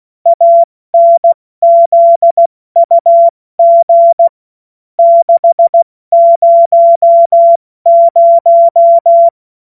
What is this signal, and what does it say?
Morse code, heard as a keyed tone, about 16 wpm, 680 Hz. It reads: ANZUG 600
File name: Witz-30-September-2025-70-F80